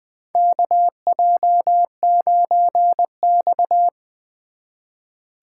Morse code, heard as a keyed tone, about 20 wpm, 700 Hz. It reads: KJ9X
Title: 2023 SOTA Chaser Call Signs - 20wpm